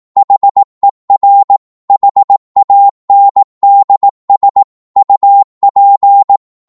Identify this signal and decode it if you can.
Morse code, heard as a keyed tone, about 18 wpm, 800 Hz. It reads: HER HANDS UP
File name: 18wpm